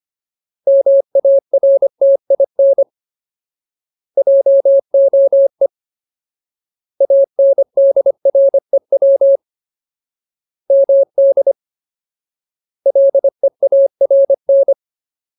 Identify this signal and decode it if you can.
Morse code, heard as a keyed tone, about 25 wpm, 550 Hz. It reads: MARTIN JOE ANDREW MD LEARN
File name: Ultimate Word List / 25WPM Ultimate